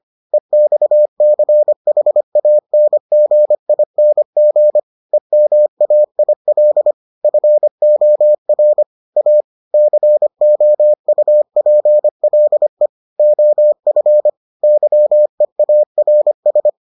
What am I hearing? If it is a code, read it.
Morse code, heard as a keyed tone, about 25 wpm, 600 Hz. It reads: EXCHANGING EMAIL FOR A COUPLE OF YEARS